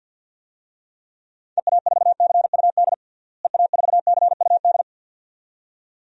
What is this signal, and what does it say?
Morse code, heard as a keyed tone, about 49 wpm, 700 Hz. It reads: EA4CUD EA4CUD
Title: DXCC-100 Call Signs - 6 Characters - Repeated 2x - 50wpm